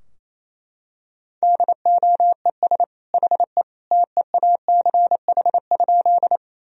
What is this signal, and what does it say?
Morse code, heard as a keyed tone, about 28 wpm, 690 Hz. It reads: DOES HE TEACH?
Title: Sentences from Top 1000 Words - Review Part 2 - 28wpm